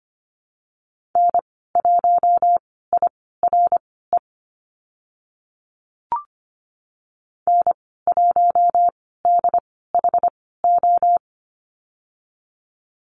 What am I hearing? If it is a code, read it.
Morse code, heard as a keyed tone, about 25 wpm, spaced as for 16 wpm, 700 Hz. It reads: N1IRE N1DHO
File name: Lesson 15g - Number 5 - Call Signs 25wpm